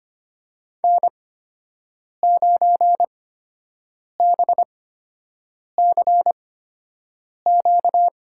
Morse code, heard as a keyed tone, about 25 wpm, spaced as for 7 wpm, 700 Hz. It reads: N9BCQ
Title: USA Call Signs - 5 Characters - 8x Character Spacing - 25wpm